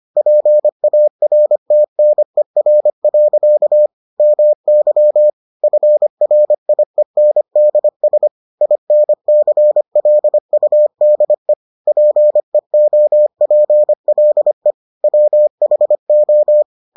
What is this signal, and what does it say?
Morse code, heard as a keyed tone, about 25 wpm, 600 Hz. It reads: PARTNER. MY FRIENDS INCLUDE PEOPLE WHO